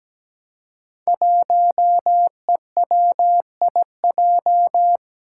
Morse code, heard as a keyed tone, about 17 wpm, 700 Hz. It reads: 1EWIJ
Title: Five Challenging Letter-Number Combinations 17wpm